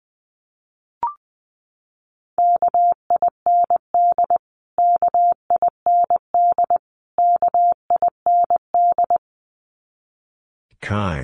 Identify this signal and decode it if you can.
Morse code, heard as a keyed tone, about 20 wpm, 700 Hz. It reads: KIND KIND KIND
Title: Top 200 Words Repeated 20wpm